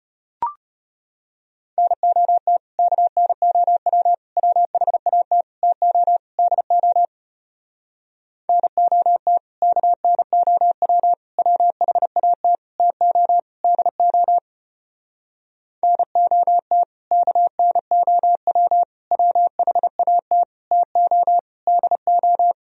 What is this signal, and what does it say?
Morse code, heard as a keyed tone, about 33 wpm, 700 Hz. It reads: NOT KNOW WHAT TO DO NOT KNOW WHAT TO DO NOT KNOW WHAT TO DO